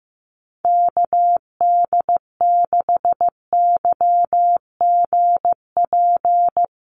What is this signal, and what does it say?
Morse code, heard as a keyed tone, about 15 wpm, 700 Hz. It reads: KD6YGP
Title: USA Call Signs Repeated 3x 15wpm